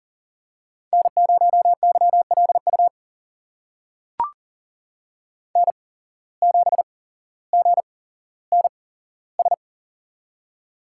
Morse code, heard as a keyed tone, about 39 wpm, spaced as for 17 wpm, 700 Hz. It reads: N0YLU N7GNS
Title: USA Call Signs - 5 Characters - 8x Character Spacing - 40wpm